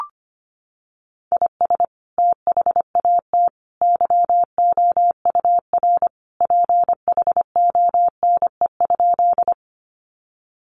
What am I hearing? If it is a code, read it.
Morse code, heard as a keyed tone, about 25 wpm, 700 Hz. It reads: IS THAT YOUR PHONE?
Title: Sentences from Top 1100 Words 25wpm